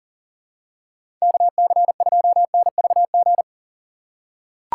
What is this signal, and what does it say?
Morse code, heard as a keyed tone, about 39 wpm, 700 Hz. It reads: KC2NVG
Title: USA Call Signs - 6 Characters - 4x Character Spacing - 40wpm